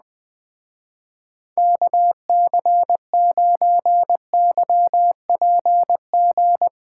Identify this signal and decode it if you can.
Morse code, heard as a keyed tone, about 21 wpm, 700 Hz. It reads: KC9YPG
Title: Speed Racing - USA 2x3 Call Signs 17wpm